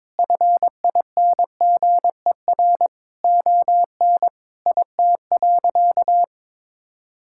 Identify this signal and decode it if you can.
Morse code, heard as a keyed tone, about 21 wpm, 700 Hz. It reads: FINGER ON IT.
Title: Sentences from Top 1600 Words 22wpm